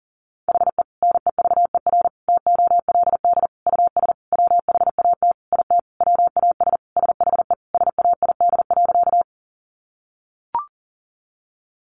Morse code, heard as a keyed tone, about 40 wpm, 700 Hz. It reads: HE NEVER TOLD US WHAT IT WAS SHE SAID.
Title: Sentences from Top 400 Words 40wpm